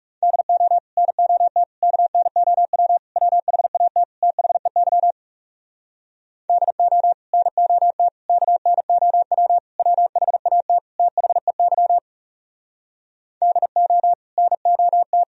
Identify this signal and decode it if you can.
Morse code, heard as a keyed tone, about 37 wpm, 700 Hz. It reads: DO NOT KNOW WHAT THEY DO NOT KNOW WHAT THEY DO NOT
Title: Speed-Racing - Sets of 5 Words 30wpm